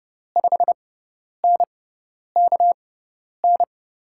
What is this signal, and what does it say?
Morse code, heard as a keyed tone, about 32 wpm, spaced as for 10 wpm, 700 Hz. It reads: 5NKN
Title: USA Call Signs - 6 Characters - 6x Character Spacing - 30wpm